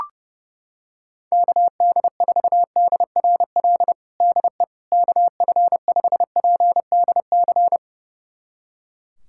Morse code, heard as a keyed tone, about 30 wpm, 700 Hz. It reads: KD4DRL DE KF5PDC
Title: USA Callsign DE Callsign - 6 Characters - 30wpm